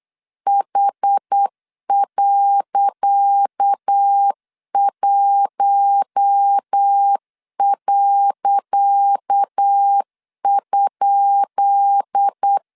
Morse code, heard as a keyed tone, about 8 wpm, 800 Hz. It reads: H.1.?
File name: LEZIONI / 10 SIGNS / 10C DOT 8wpm 3649